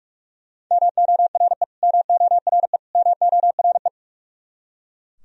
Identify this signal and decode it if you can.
Morse code, heard as a keyed tone, about 42 wpm, 700 Hz. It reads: MORE MORE MORE
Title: Top 100 Words Repeated 45wpm